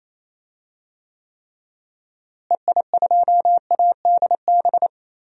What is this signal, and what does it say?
Morse code, heard as a keyed tone, about 28 wpm, 700 Hz. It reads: EI2ADB
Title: DXCC-100 Call Signs - 6 Characters - Repeated 3x - 28wpm